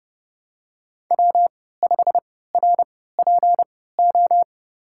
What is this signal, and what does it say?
Morse code, heard as a keyed tone, about 30 wpm, spaced as for 17 wpm, 700 Hz. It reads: W5RPO
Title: USA Call Signs - 5 Characters - 3x Character Spacing - 30wpm